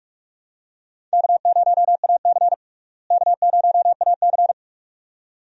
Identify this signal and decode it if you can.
Morse code, heard as a keyed tone, about 46 wpm, 700 Hz. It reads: K0AC K0AC
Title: USA Call Signs - 4 Characters - Repeated 2x - 45wpm